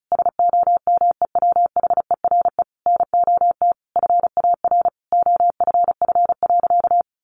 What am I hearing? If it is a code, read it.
Morse code, heard as a keyed tone, about 34 wpm, 700 Hz. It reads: SOMEWHERE NOT FAR OFF.
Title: Sentences from Top 1400 Words 35wpm